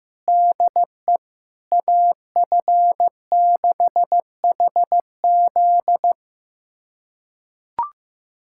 Morse code, heard as a keyed tone, about 16 wpm, 700 Hz. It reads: DE AF6HZ
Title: USA Callsign DE Callsign 15wpm